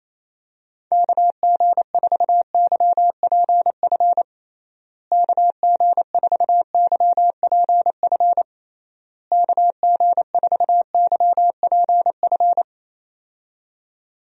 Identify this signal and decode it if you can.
Morse code, heard as a keyed tone, about 28 wpm, 700 Hz. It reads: KG4YPF KG4YPF KG4YPF
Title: USA Call Signs - 6 Characters - Repeated 3x - 28wpm